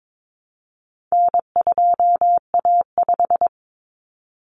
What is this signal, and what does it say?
Morse code, heard as a keyed tone, about 22 wpm, 700 Hz. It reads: N2A5
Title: Common Cut Numbers - 4-Digit 22wpm